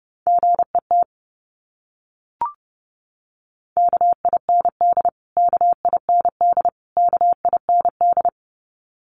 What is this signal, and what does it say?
Morse code, heard as a keyed tone, about 30 wpm, 700 Hz. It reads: GET KIND KIND KIND
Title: Top 500 Words Repeated 30wpm